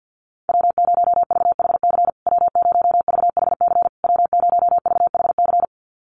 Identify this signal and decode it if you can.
Morse code, heard as a keyed tone, about 49 wpm, 700 Hz. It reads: W0VHC W0VHC W0VHC
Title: USA Call Signs Repeated 3x 50wpm